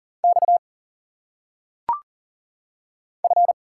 Morse code, heard as a keyed tone, about 40 wpm, 700 Hz. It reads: X F
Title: Single Letters - Phonetic Pronunciation 40wpm